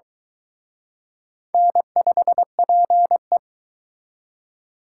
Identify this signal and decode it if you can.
Morse code, heard as a keyed tone, about 22 wpm, 700 Hz. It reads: N5PE